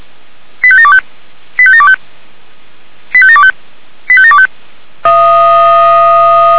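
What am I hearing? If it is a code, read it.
Five-tone selective call (ZVEI), heard as quick runs of five tones, tones of about 70 ms. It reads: 85725 85725 85725 85725
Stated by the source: ZVEI five-tone selective call received on an SDR.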